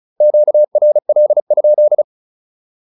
Morse code, heard as a keyed tone, about 36 wpm, 600 Hz. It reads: QRL?
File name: QSO102 35